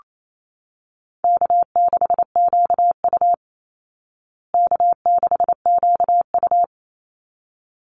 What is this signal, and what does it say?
Morse code, heard as a keyed tone, about 28 wpm, 700 Hz. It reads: K6QU K6QU
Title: Easy Call Signs Repeated 2x - 28wpm